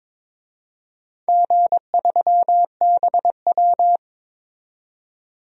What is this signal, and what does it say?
Morse code, heard as a keyed tone, about 22 wpm, 700 Hz. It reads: G3BW
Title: DXCC-100 Call Signs - 4 Characters - 22wpm